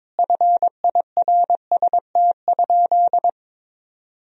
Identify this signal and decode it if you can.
Morse code, heard as a keyed tone, about 22 wpm, 700 Hz. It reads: FIRST?
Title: Sentences from Top 1000 Words - Review Part 3 - 22wpm